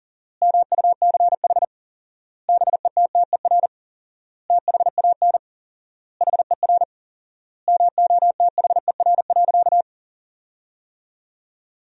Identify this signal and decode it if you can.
Morse code, heard as a keyed tone, about 40 wpm, 700 Hz. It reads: MUCH BETTER THAN HER MOTHER.
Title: 4x Word-Spacing - Sentences from Top 400 Words - 40wpm